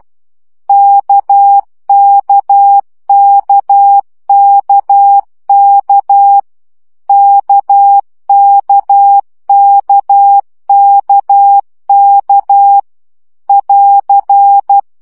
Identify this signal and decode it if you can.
Morse code, heard as a keyed tone, about 12 wpm, 800 Hz. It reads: KKKKK KKKKK <AR>
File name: CW gyakorlas k 10 baud